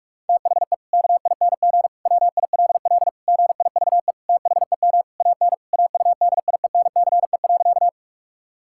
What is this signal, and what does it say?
Morse code, heard as a keyed tone, about 46 wpm, 700 Hz. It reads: THE KING WILL GIVE THEM AN AUDIENCE.
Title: Sentences from Top 1000 Words - Review Part 5 - 45wpm